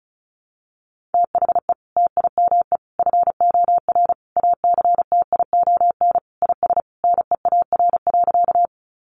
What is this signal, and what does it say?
Morse code, heard as a keyed tone, about 34 wpm, 700 Hz. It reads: THE TIME FOR ACTION IS NEAR.
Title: Sentences from Top 700 Words 35wpm